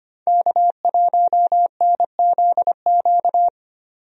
Morse code, heard as a keyed tone, about 25 wpm, 700 Hz. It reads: K1NZQ
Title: USA Call Signs - 5 Characters - 25wpm